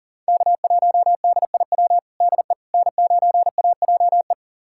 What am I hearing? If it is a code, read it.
Morse code, heard as a keyed tone, about 40 wpm, 700 Hz. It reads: K1DIW DE N9AJE
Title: USA Callsign DE Callsign - 5 Characters - 40wpm